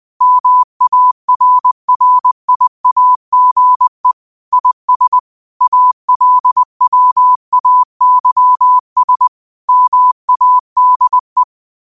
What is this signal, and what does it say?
Morse code, heard as a keyed tone, about 20 wpm, 1000 Hz. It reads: MARRIAGE IS ALWAYS MADE
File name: qotdfiles / 20wpm